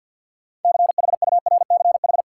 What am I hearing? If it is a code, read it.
Morse code, heard as a keyed tone, about 53 wpm, 700 Hz. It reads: CHURCH
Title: Sets of 5 Words - New Word Drill - 50wpm